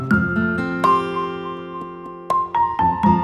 There is no code.